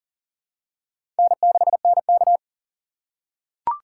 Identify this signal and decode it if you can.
Morse code, heard as a keyed tone, about 40 wpm, 700 Hz. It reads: N6NK
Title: USA Call Signs - 4 Characters - 3x Character Spacing - 40wpm